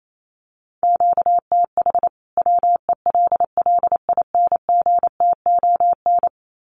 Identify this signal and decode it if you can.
Morse code, heard as a keyed tone, about 28 wpm, 700 Hz. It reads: QTH WELLINGTON
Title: Common QSO Phrases 28wpm